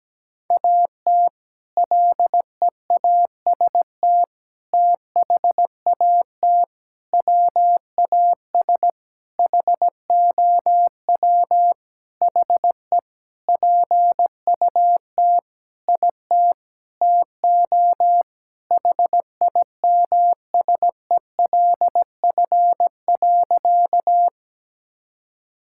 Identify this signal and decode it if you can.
Morse code, heard as a keyed tone, about 17 wpm, 700 Hz. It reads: AT LEAST THAT WAS HOW HE PUT IT TO HIMSELF.
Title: Sentences from Top 1600 Words 17wpm